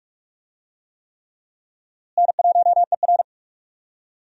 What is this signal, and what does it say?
Morse code, heard as a keyed tone, about 42 wpm, 700 Hz. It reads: N1ER
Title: USA Call Signs - 4 Characters - Repeated 3x - 45wpm